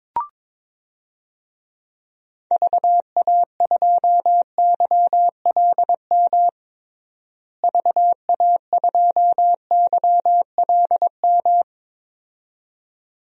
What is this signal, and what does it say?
Morse code, heard as a keyed tone, about 22 wpm, 700 Hz. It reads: VA2YLM VA2YLM
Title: DXCC-100 Call Signs - 6 Characters - Repeated 2x - 22wpm